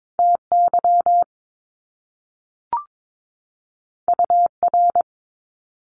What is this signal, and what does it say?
Morse code, heard as a keyed tone, about 22 wpm, 700 Hz. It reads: TY UR
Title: Top Two Letter Combinations in Words 22wpm